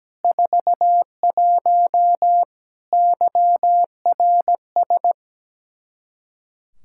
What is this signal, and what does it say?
Morse code, heard as a keyed tone, about 17 wpm, 700 Hz. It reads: 41 YRS
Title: Practice QSO - Exchange 3 - More Info - 17wpm